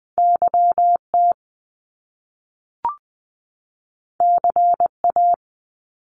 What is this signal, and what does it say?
Morse code, heard as a keyed tone, about 20 wpm, 700 Hz. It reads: YT CA
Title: Top Two Letter Combinations in Words 20wpm